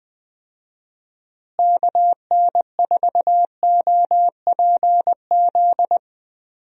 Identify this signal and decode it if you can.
Morse code, heard as a keyed tone, about 20 wpm, 700 Hz. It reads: KN4OPZ
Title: USA Call Signs - 6 Characters - Repeated 2x - 20wpm